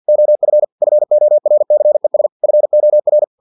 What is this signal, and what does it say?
Morse code, heard as a keyed tone, about 50 wpm, 600 Hz. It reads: OF FORCES FOR
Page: Never 49wpm